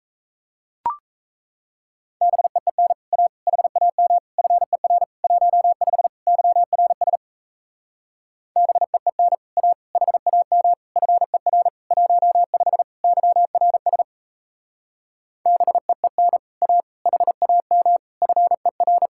Morse code, heard as a keyed tone, about 37 wpm, 700 Hz. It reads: BEEN A HAM FER 15 YRS BEEN A HAM FER 15 YRS BEEN A HAM FER